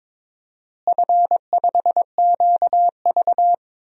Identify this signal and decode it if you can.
Morse code, heard as a keyed tone, about 22 wpm, 700 Hz. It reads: F5QV
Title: DXCC-100 Call Signs - 4 Characters - Repeated 3x - 22wpm